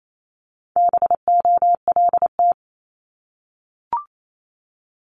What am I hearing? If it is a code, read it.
Morse code, heard as a keyed tone, about 28 wpm, 700 Hz. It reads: BOLT
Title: Multi-letter Phonemes Repeated 3x 28wpm